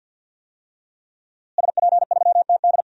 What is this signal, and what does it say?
Morse code, heard as a keyed tone, about 52 wpm, 700 Hz. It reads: SP3TD
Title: DXCC-100 Call Signs - 5 Characters - Repeated 3x - 50wpm